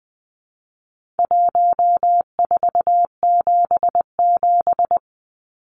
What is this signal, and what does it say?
Morse code, heard as a keyed tone, about 20 wpm, 700 Hz. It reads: 1477
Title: Numbers - Contest Style - 4-Digit 20wpm